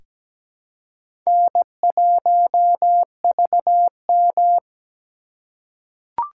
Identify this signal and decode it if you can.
Morse code, heard as a keyed tone, about 17 wpm, 700 Hz. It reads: N1VM
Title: USA Call Signs - 4 Characters - 4x Character Spacing - 17wpm